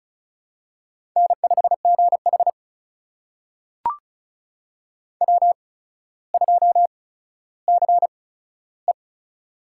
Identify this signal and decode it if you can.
Morse code, heard as a keyed tone, about 36 wpm, spaced as for 15 wpm, 700 Hz. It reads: N5GH W2CE
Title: USA Call Signs - 4 Characters - 8x Character Spacing - 35wpm